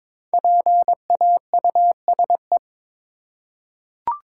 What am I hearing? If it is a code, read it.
Morse code, heard as a keyed tone, about 22 wpm, 690 Hz. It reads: PAUSE
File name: 5 Letter Words - First 3 Letters Given - 22wpm